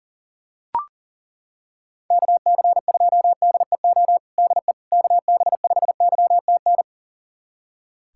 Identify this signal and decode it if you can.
Morse code, heard as a keyed tone, about 40 wpm, 700 Hz. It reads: KC2DEO DE KB5YTN
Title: USA Callsign DE Callsign 40wpm